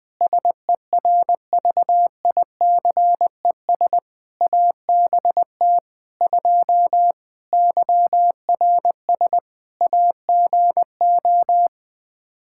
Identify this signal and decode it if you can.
Morse code, heard as a keyed tone, about 21 wpm, 700 Hz. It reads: SERVICES ABT 2 YRS AGO
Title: Practice QSO - Exchange 3 - More Info - 20wpm